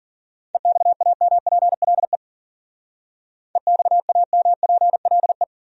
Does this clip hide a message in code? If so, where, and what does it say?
Yes: Morse code from 0.5 to 5.4 s, heard as a keyed tone, about 52 wpm, 700 Hz. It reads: EXAMPLE EXAMPLE